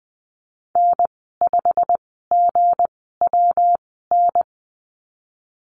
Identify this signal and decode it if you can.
Morse code, heard as a keyed tone, about 20 wpm, spaced as for 14 wpm, 700 Hz. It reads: N5GWN
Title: 173 Lesson 28g - Number 9 - Call Signs 20wpm